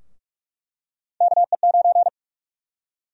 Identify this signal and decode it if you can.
Morse code, heard as a keyed tone, about 43 wpm, 700 Hz. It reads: KE9
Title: USA Call Sign Prefixes - 45wpm